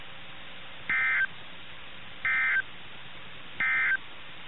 Emergency Alert System (SAME) tones, heard as harsh buzzing bursts. It is heard from 0.8 to 4.1 s.
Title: NOAA Weather Radio - KIH61